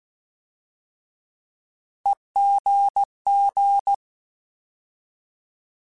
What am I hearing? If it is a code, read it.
Morse code, heard as a keyed tone, about 16 wpm, 800 Hz. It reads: EGG
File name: Alphabetical / Alphabetical mp3s / Es @15